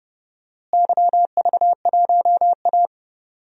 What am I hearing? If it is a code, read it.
Morse code, heard as a keyed tone, about 30 wpm, 700 Hz. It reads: YV1A
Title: DXCC-100 Call Signs - 4 Characters - Repeated 3x - 30wpm